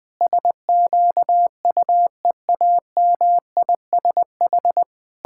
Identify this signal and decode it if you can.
Morse code, heard as a keyed tone, about 20 wpm, 700 Hz. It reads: SQUEAMISH
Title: Pangram Challenge v2 - 20wpm